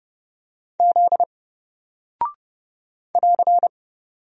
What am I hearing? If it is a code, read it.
Morse code, heard as a keyed tone, about 30 wpm, 700 Hz. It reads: Z <AR>
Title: Single Character (Letter:Number:Punctuation:Prosign) 30wpm